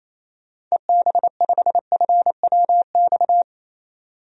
Morse code, heard as a keyed tone, about 28 wpm, 700 Hz. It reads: EB5FWX
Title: DXCC-100 Call Signs - 6 Characters - 28wpm